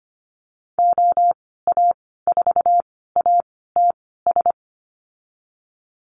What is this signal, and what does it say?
Morse code, heard as a keyed tone, about 25 wpm, spaced as for 16 wpm, 700 Hz. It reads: OA4ATS
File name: Lesson 15g - Number 5 - Call Signs 25wpm